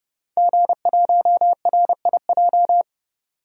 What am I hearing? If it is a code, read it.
Morse code, heard as a keyed tone, about 32 wpm, 700 Hz. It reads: G1RIJ
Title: DXCC-100 Call Signs - 5 Characters - 30wpm